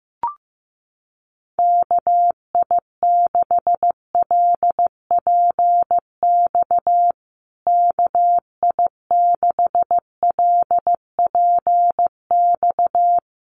Morse code, heard as a keyed tone, about 15 wpm, 700 Hz. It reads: KI6LPX KI6LPX
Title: USA Call Signs Repeated 3x 15wpm